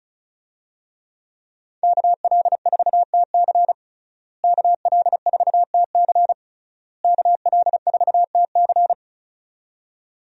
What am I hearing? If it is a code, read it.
Morse code, heard as a keyed tone, about 36 wpm, 700 Hz. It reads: KL4TC KL4TC KL4TC
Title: USA Call Signs - 5 Characters - Repeated 3x - 35wpm